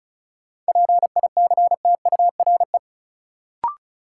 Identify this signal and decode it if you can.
Morse code, heard as a keyed tone, about 34 wpm, 700 Hz. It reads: PICTURE
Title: Sentences from Top 500 Words - New Word Drill - 35wpm